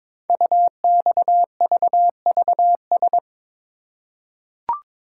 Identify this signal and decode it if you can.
Morse code, heard as a keyed tone, about 22 wpm, 700 Hz. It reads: UXVVS
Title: Code Groups - 5 Letters - 22wpm